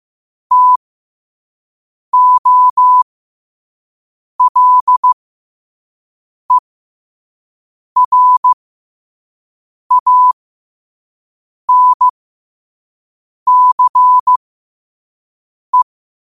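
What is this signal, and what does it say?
Morse code, heard as a keyed tone, about 15 wpm, spaced as for 5 wpm, 1000 Hz. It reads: TOLERANCE
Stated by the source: Morse code sent at 5 wpm.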